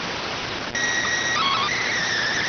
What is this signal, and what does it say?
SSTV (slow-scan television), PD 290.